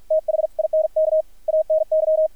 Morse code, heard as a keyed tone, about 53 wpm, spaced as for 46 wpm, 630 Hz, with radio noise. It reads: THINK ANY
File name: archived / CRSnet